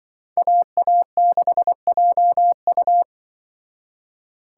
Morse code, heard as a keyed tone, about 25 wpm, 700 Hz. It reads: AA6JU
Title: Speed Racing - USA Call Signs 20wpm